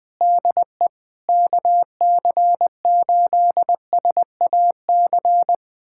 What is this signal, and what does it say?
Morse code, heard as a keyed tone, about 20 wpm, 700 Hz. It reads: DE KC8SAC
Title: USA Callsign DE Callsign 20wpm